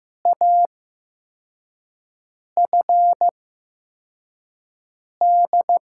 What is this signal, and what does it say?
Morse code, heard as a keyed tone, about 16 wpm, spaced as for 4 wpm, 700 Hz. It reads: AFD
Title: USA Call Signs - 8x Character Spacing - 15wpm